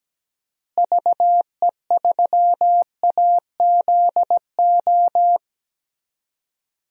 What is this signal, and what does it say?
Morse code, heard as a keyed tone, about 17 wpm, 700 Hz. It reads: VE3AZO
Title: DXCC-100 Call Signs - 6 Characters - Repeated 2x - 17wpm